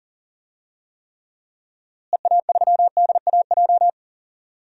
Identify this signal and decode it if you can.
Morse code, heard as a keyed tone, about 39 wpm, 700 Hz. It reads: EA3DAJ
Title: DXCC-100 Call Signs - 6 Characters - Repeated 3x - 40wpm